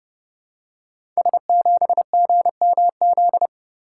Morse code, heard as a keyed tone, about 31 wpm, 700 Hz. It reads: S7GMZ
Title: Code Groups - 5 Letter:Numbers - 30wpm